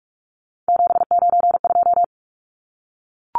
Morse code, heard as a keyed tone, about 44 wpm, 700 Hz. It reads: 792
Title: Numbers - 3-Digit 45wpm